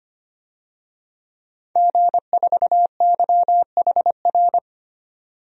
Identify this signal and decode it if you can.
Morse code, heard as a keyed tone, about 25 wpm, 700 Hz. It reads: G4YHR
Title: DXCC-100 Call Signs - 5 Characters - 25wpm